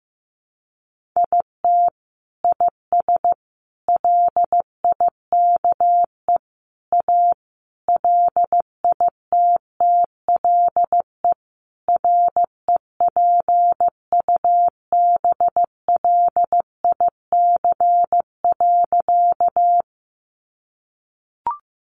Morse code, heard as a keyed tone, about 15 wpm, 700 Hz. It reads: IT IS LIKE A LITTLE REPUBLIC.
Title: Sentences from Top 1500 Words 15wpm